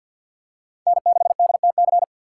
Morse code, heard as a keyed tone, about 48 wpm, 700 Hz. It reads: N6DTC
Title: USA Call Signs - 5 Characters - 2x Character Spacing - 50wpm